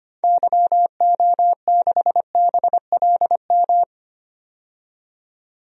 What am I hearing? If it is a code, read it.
Morse code, heard as a keyed tone, about 25 wpm, 700 Hz. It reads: YO6BLM
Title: DXCC-100 Call Signs - 6 Characters - 25wpm